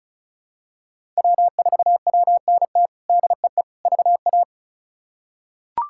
Morse code, heard as a keyed tone, about 34 wpm, 700 Hz. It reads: W4WNT DEE VA
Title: K1USN SST QSO Practice - 35wpm